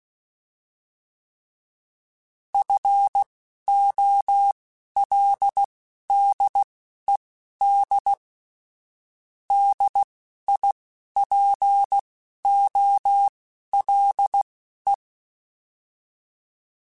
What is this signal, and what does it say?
Morse code, heard as a keyed tone, about 16 wpm, spaced as for 12 wpm, 800 Hz. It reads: FOLDED DIPOLE